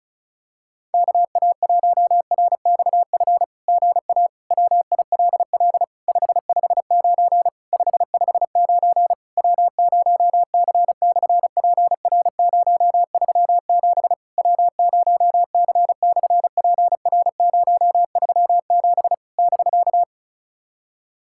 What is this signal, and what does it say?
Morse code, heard as a keyed tone, about 33 wpm, 700 Hz. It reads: KA1RXF GA WILL 559 559 W0C/PR037 W0C/PR037 <BK>
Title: SOTA QSO Practice 35wpm